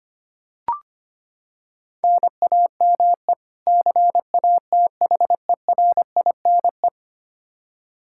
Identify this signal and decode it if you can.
Morse code, heard as a keyed tone, about 25 wpm, 700 Hz. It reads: NAME CATHERINE
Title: Practice QSO - Exchange 1 - Beginning - 25wpm